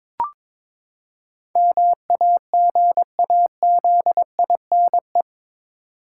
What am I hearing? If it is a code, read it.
Morse code, heard as a keyed tone, about 22 wpm, 700 Hz. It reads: MAGAZINE
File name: practice / 22wpm / Eight Letter Words - 22wpm